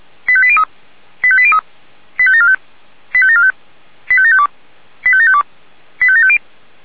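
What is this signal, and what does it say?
Five-tone selective call (ZVEI), heard as quick runs of five tones, tones of about 70 ms. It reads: 85803 85803 85745 85745 85752 85752 85750
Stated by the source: ZVEI five-tone selective call received with an SDR.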